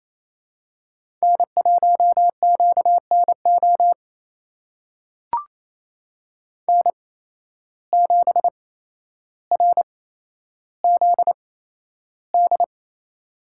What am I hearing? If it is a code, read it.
Morse code, heard as a keyed tone, about 28 wpm, spaced as for 12 wpm, 700 Hz. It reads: N1QNO N7RZD
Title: USA Call Signs - 5 Characters - 8x Character Spacing - 28wpm